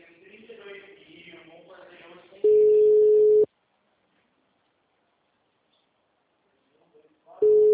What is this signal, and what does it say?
Ringback tone (Europe, 425 Hz), with speech.
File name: out- -2014- - - .11581